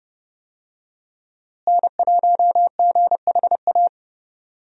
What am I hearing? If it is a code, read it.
Morse code, heard as a keyed tone, about 31 wpm, 700 Hz. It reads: N1GHA
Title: USA Call Signs - 5 Characters - Repeated 2x - 30wpm